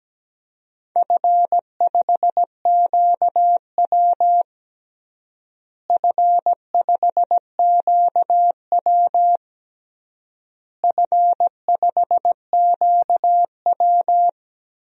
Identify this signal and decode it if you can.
Morse code, heard as a keyed tone, about 17 wpm, 700 Hz. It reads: F5QW F5QW F5QW
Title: DXCC-100 Call Signs - 4 Characters - Repeated 3x - 17wpm